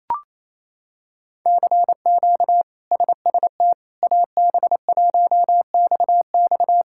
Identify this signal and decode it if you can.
Morse code, heard as a keyed tone, about 28 wpm, 700 Hz. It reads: CQ SST AB1XX
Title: K1USN SST QSO Practice - 28wpm